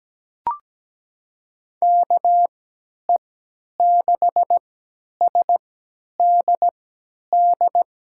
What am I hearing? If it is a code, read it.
Morse code, heard as a keyed tone, about 17 wpm, spaced as for 10 wpm, 700 Hz. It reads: KE6SDD
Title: USA Call Signs - 6 Characters - 3x Character Spacing - 17wpm